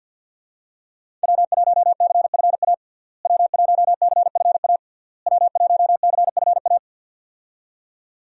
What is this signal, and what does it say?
Morse code, heard as a keyed tone, about 46 wpm, 700 Hz. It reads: W1CFA W1CFA W1CFA
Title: USA Call Signs - 5 Characters - Repeated 3x - 50wpm